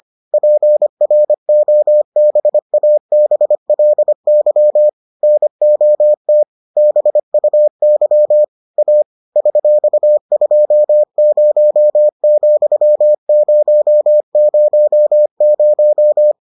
Morse code, heard as a keyed tone, about 25 wpm, 600 Hz. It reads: PROBABLY NOT BUY A $20,000